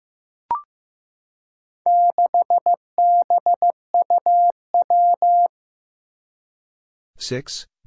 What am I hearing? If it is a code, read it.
Morse code, heard as a keyed tone, about 15 wpm, 700 Hz. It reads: 6BUW
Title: Four Challenging Letter-Number Combinations 15wpm